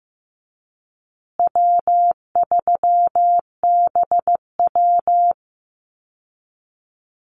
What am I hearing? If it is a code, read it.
Morse code, heard as a keyed tone, about 15 wpm, 700 Hz. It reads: W3BW
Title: Easy Call Signs Repeated 3x - 15wpm